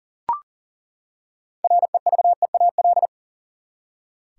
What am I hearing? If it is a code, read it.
Morse code, heard as a keyed tone, about 40 wpm, 700 Hz. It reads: REVEAL
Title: Six Letter Words - 40wpm